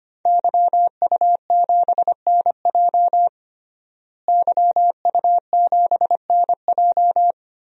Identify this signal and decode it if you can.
Morse code, heard as a keyed tone, about 25 wpm, 700 Hz. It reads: YU7NJ YU7NJ
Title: DXCC-100 Call Signs - 5 Characters - Repeated 2x - 25wpm